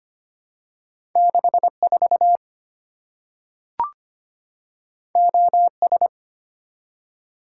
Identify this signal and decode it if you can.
Morse code, heard as a keyed tone, about 25 wpm, 700 Hz. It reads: 64 OS
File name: Two Letter-Number Combinations - 25wpm